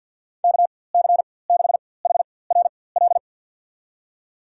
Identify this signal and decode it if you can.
Morse code, heard as a keyed tone, about 46 wpm, spaced as for 23 wpm, 700 Hz. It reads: KC6HRL
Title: USA Call Signs - 6 Characters - 4x Character Spacing - 50wpm